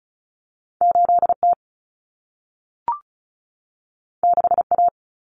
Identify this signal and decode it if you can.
Morse code, heard as a keyed tone, about 36 wpm, 700 Hz. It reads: 8T 6A
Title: Common Cut Numbers - 2-Digit 35wpm